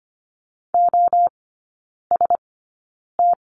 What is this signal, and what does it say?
Morse code, heard as a keyed tone, about 25 wpm, spaced as for 9 wpm, 700 Hz. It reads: OST
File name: course / 25:10wpm / 093 Lesson 16e - Letter c - Three Characters 25wpm